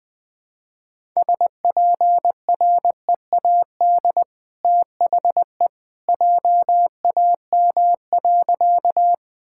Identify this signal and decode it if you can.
Morse code, heard as a keyed tone, about 20 wpm, 700 Hz. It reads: SPREAD THE JAM.
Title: Easy Sentences with Hints - 20wpm